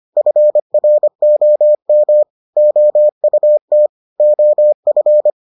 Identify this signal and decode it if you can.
Morse code, heard as a keyed tone, about 25 wpm, 600 Hz. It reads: FROM OUT OF